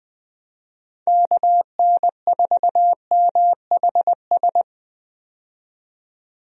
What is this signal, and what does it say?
Morse code, heard as a keyed tone, about 21 wpm, 700 Hz. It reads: KN4MHS
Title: USA Call Signs - 6 Characters - 20wpm